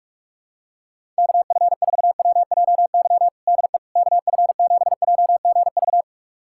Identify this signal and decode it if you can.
Morse code, heard as a keyed tone, about 46 wpm, 700 Hz. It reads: KF4WJY DE KF7JGV
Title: USA Callsign DE Callsign 45wpm